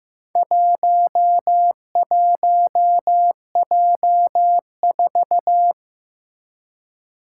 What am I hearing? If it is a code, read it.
Morse code, heard as a keyed tone, about 15 wpm, 700 Hz. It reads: 11J4
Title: Four Challenging Letter-Number Combinations 15wpm